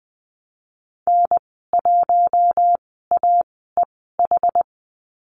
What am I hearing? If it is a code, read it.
Morse code, heard as a keyed tone, about 20 wpm, spaced as for 15 wpm, 700 Hz. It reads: N1AEH
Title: 076 Lesson 13g - Letter L - Call Signs 20wpm